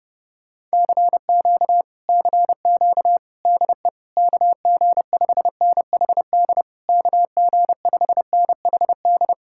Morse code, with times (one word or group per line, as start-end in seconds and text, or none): 0.7-1.8 CQ
2.1-3.2 CQ
3.5-3.9 DE
4.2-6.6 KG5NHD
6.9-9.3 KG5NHD